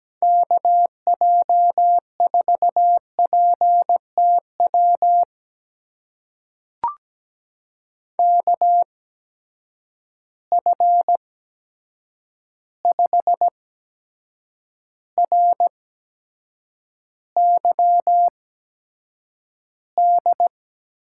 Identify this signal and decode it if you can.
Morse code, heard as a keyed tone, about 17 wpm, spaced as for 7 wpm, 700 Hz. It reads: KJ4PTW KF5RYD
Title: USA Call Signs - 6 Characters - 8x Character Spacing - 17wpm